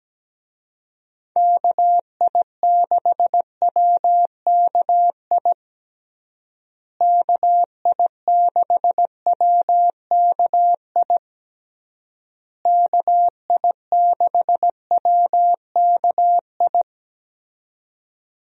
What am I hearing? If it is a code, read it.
Morse code, heard as a keyed tone, about 17 wpm, 700 Hz. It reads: KI6WKI KI6WKI KI6WKI
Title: USA Call Signs - 6 Characters - Repeated 3x - 17wpm